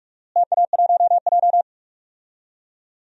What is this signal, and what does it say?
Morse code, heard as a keyed tone, about 46 wpm, 700 Hz. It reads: TA1J
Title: DXCC-100 Call Signs - 4 Characters - Repeated 3x - 45wpm